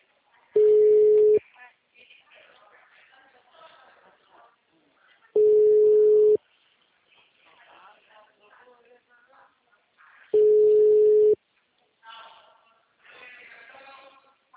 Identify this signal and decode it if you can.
Ringback tone (Europe, 425 Hz), with speech.